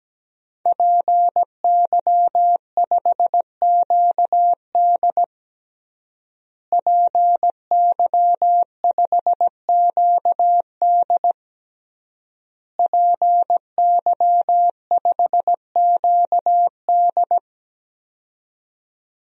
Morse code, heard as a keyed tone, about 17 wpm, 700 Hz. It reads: PY5QD PY5QD PY5QD
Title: DXCC-100 Call Signs - 5 Characters - Repeated 3x - 17wpm